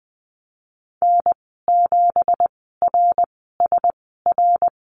Morse code, heard as a keyed tone, about 20 wpm, spaced as for 15 wpm, 700 Hz. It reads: N7RSR
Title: 167 Lesson 27g - Number 7 - Call Signs 20wpm